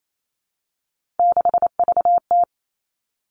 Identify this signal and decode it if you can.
Morse code, heard as a keyed tone, about 28 wpm, 700 Hz. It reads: 6VT
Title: Three Challenging Letter-Number Combinations 28wpm